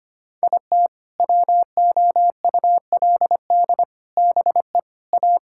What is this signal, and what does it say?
Morse code, heard as a keyed tone, about 25 wpm, 700 Hz. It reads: IT WOULD BE A
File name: practice / 25wpm / Sentences with 4-Word Phrases - 25wpm